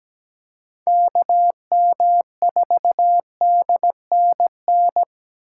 Morse code, heard as a keyed tone, about 20 wpm, 700 Hz. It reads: KM4DNN
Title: Speed Racing - USA Call Signs 17wpm